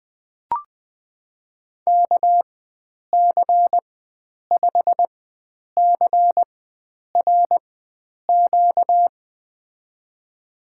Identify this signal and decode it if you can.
Morse code, heard as a keyed tone, about 21 wpm, spaced as for 9 wpm, 700 Hz. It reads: KC5CRQ
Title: USA Call Signs - 6 Characters - 4x Character Spacing - 20wpm